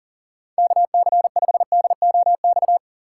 Morse code, heard as a keyed tone, about 40 wpm, 700 Hz. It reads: KC5DOX
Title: USA Call Signs - 6 Characters - 6x Character Spacing - 40wpm